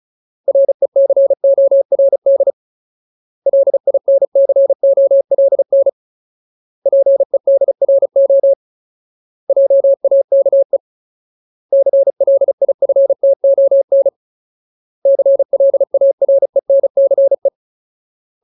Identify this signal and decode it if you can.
Morse code, heard as a keyed tone, about 34 wpm, 550 Hz. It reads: RECORD LINCOLN PEDRO JAKE CLIFTON CLARENCE
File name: Ultimate Word List / 35WPM Ultimate